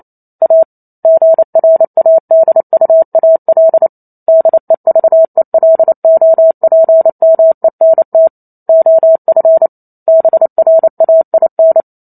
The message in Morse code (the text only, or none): A GRADUAL DEVELOPMENT OF BRAIN